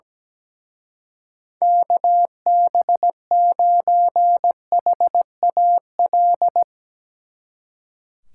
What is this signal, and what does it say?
Morse code, heard as a keyed tone, about 18 wpm, 700 Hz. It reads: KB9HAL
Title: Speed Racing - USA 2x3 Call Signs 17wpm